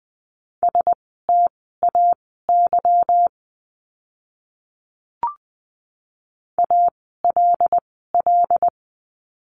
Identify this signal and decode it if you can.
Morse code, heard as a keyed tone, about 20 wpm, spaced as for 14 wpm, 700 Hz. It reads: STAY ALL
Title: 146 Lesson 24f - Letter y - Words 20wpm